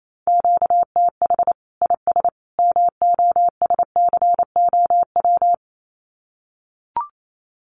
Morse code, heard as a keyed tone, about 28 wpm, 700 Hz. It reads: QTH IS MOSCOW
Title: Common QSO Phrases 28wpm